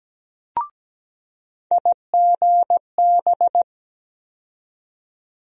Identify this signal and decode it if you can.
Morse code, heard as a keyed tone, about 17 wpm, 700 Hz. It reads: IGB
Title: USA Call Sign Postfixes - 3 Letters 17wpm